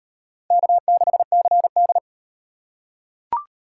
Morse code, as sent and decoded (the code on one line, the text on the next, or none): -.- -.... -.-. -..
K6CD